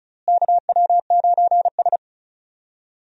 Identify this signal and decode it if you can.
Morse code, heard as a keyed tone, about 34 wpm, 700 Hz. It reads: KW9S
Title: USA Call Signs - 4 Characters - Repeated 2x - 35wpm